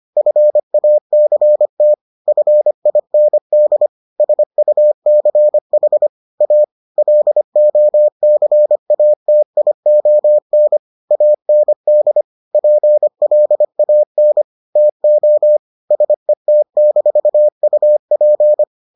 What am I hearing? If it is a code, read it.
Morse code, heard as a keyed tone, about 25 wpm, 600 Hz. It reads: FACT FIND SUCH A LOCATION AND PLAN TO SET-UP